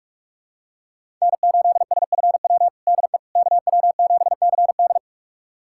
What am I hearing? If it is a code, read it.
Morse code, heard as a keyed tone, about 46 wpm, 700 Hz. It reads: N8SFW DE KW7CD
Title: USA Callsign DE Callsign - 5 Characters - 45wpm